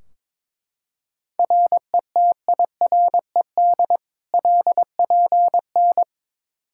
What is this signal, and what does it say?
Morse code, heard as a keyed tone, about 22 wpm, 700 Hz. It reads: RETIRED LPN
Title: Practice QSO - Exchange 3 - More Info - 22wpm